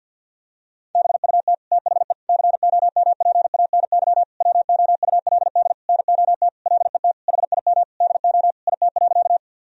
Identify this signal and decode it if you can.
Morse code, heard as a keyed tone, about 49 wpm, 700 Hz. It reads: BUT THE COMPANY WOULD NOT LET HIM DO IT.
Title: Sentences from Top 1000 Words - Review Part 3 - 50wpm